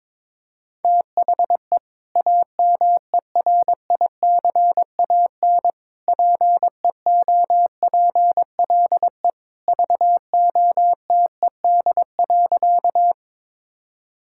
Morse code, heard as a keyed tone, about 22 wpm, 700 Hz. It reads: THE AMERICAN PEOPLE VOTED.
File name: practice / 22wpm / Sentences with 3-Word Phrases - 22wpm